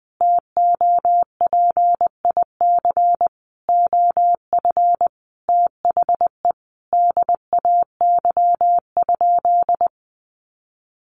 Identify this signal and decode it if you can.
Morse code, heard as a keyed tone, about 20 wpm, 700 Hz. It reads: TOPIC OF THE DAY?
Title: Sentences from Top 1200 Words 20wpm